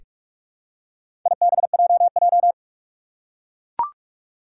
Morse code, heard as a keyed tone, about 42 wpm, 700 Hz. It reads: IBJJ
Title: Four Challenging Letter-Number Combinations 45wpm